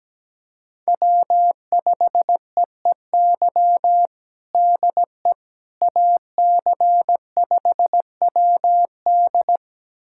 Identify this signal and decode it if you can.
Morse code, heard as a keyed tone, about 17 wpm, 700 Hz. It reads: W5EEY DE AC5WD
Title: USA Callsign DE Callsign - 5 Characters - 17wpm